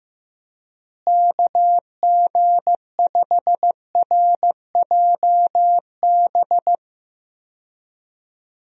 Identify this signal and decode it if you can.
Morse code, heard as a keyed tone, about 15 wpm, 700 Hz. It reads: KG5RJB
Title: USA Call Signs - 6 Characters - Repeated 3x - 15wpm